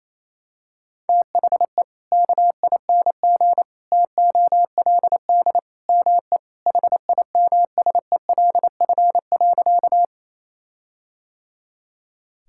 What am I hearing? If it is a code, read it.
Morse code, heard as a keyed tone, about 28 wpm, 700 Hz. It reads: THE KING TOLD ME HIMSELF.
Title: Sentences from Top 1600 Words 28wpm